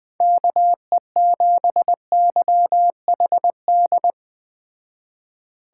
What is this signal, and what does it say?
Morse code, heard as a keyed tone, about 21 wpm, 700 Hz. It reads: KE7YHD